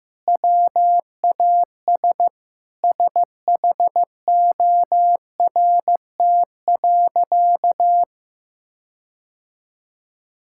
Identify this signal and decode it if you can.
Morse code, heard as a keyed tone, about 16 wpm, 700 Hz. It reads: WAS SHORT.
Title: Sentences from Top 1600 Words 15wpm